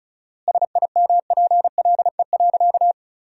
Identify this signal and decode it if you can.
Morse code, heard as a keyed tone, about 36 wpm, 700 Hz. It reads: SIMPLE.
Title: 4x Word-Spacing - Sentences from Top 500 Words - 35wpm